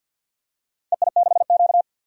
Morse code, heard as a keyed tone, about 51 wpm, 700 Hz. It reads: EI6Q
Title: DXCC-100 Call Signs - 4 Characters - 50wpm